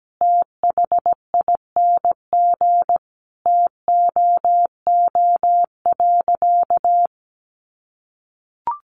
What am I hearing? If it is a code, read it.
Morse code, heard as a keyed tone, about 17 wpm, 700 Hz. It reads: THING TOO.
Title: Sentences from Top 500 Words 17wpm